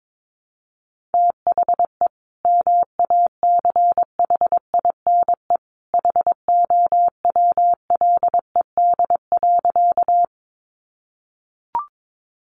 Morse code, heard as a keyed tone, about 22 wpm, 700 Hz. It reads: THE MACHINE HOWLED.
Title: A Sound Of Thunder 22wpm